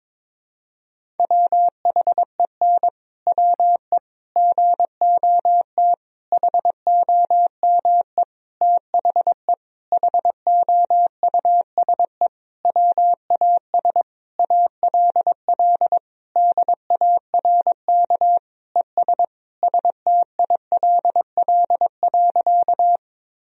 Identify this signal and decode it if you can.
Morse code, heard as a keyed tone, about 22 wpm, 700 Hz. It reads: WHEN WE GOT HOME THE HOUSE WAS ALL DARK ES STILL.
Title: Sentences from Top 500 Words and Abbreviations 22wpm